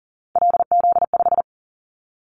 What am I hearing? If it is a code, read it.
Morse code, heard as a keyed tone, about 42 wpm, 700 Hz. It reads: LZ5
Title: Callsign Prefixes for Top 100 DXCC - 40wpm